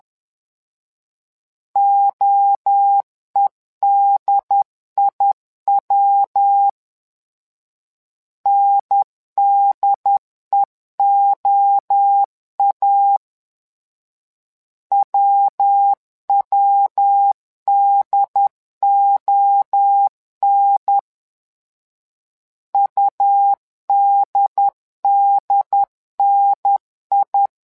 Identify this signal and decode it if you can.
Morse code, heard as a keyed tone, about 10 wpm, 800 Hz. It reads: OEDIW NDEOA WWDON UDDNI
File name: LEZIONI / 3 MWRO / 3 MWRO 10wpm 2148